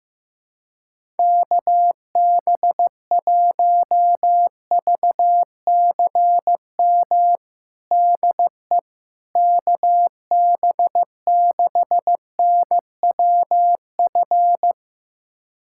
Morse code, heard as a keyed tone, about 15 wpm, 700 Hz. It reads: KB1VCM DE KB6NWF
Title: USA Callsign DE Callsign - 6 Characters - 15wpm